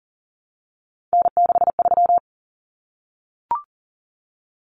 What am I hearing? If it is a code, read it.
Morse code, heard as a keyed tone, about 40 wpm, 700 Hz. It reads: N63
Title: Common Cut Numbers - 3-Digit 40wpm